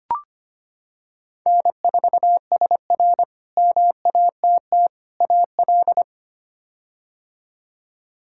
Morse code, heard as a keyed tone, about 25 wpm, 700 Hz. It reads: N4SR MATT AL
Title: K1USN SST QSO Practice - 25wpm